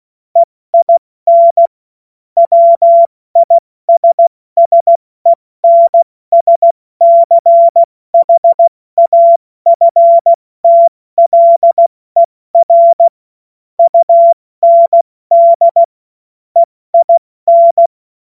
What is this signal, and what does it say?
Morse code, heard as a keyed tone, about 16 wpm, 680 Hz. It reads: EIN WISSENSCHAFTLER UND EIN
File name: Witz-23-September-2025-70-F80